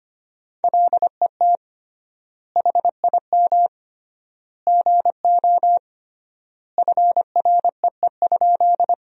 Morse code, heard as a keyed tone, about 25 wpm, 700 Hz. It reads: LET HIM GO FREE?
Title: 3x Word-Spacing - Sentences from Top 500 Words - 25wpm